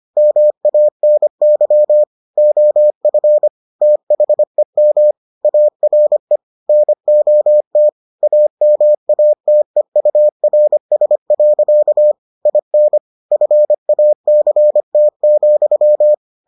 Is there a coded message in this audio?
Yes: Morse code from 0.2 to 16.1 s, heard as a keyed tone, about 25 wpm, 600 Hz. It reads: MANY OF THEM ARE NOT AMATEURS. IN FACT,